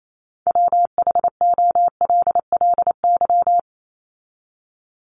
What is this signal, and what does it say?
Morse code, heard as a keyed tone, about 28 wpm, 700 Hz. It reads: WHOLLY
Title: LY Suffix - Top 100 Words with Base Words 28wpm